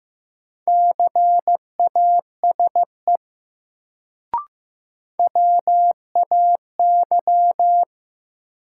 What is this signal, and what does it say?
Morse code, heard as a keyed tone, about 15 wpm, 700 Hz. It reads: CASE WAY
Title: Sets of 2 Words - New Word Drill - 15wpm